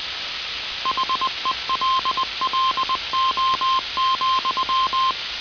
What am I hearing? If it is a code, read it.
Morse code, heard as a keyed tone, about 20 wpm, 1050 Hz, with noise. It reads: HELLO,